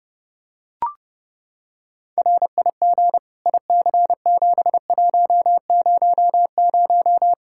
Morse code, heard as a keyed tone, about 30 wpm, 700 Hz. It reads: RIG IC7100
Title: Practice QSO - Exchange 2 - Info - 30wpm